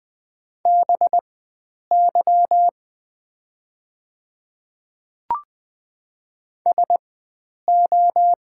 Morse code, heard as a keyed tone, about 20 wpm, spaced as for 9 wpm, 700 Hz. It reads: BY SO
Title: 4x Character Spacing - Top 100 Words - 20wpm